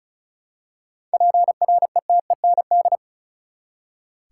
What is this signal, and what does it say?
Morse code, heard as a keyed tone, about 36 wpm, 700 Hz. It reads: PRETEND
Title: Seven Letter Words - 35wpm